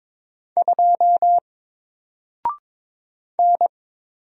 Morse code, heard as a keyed tone, about 22 wpm, 700 Hz. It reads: 2 N
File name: Single-Letter-Number-22wpm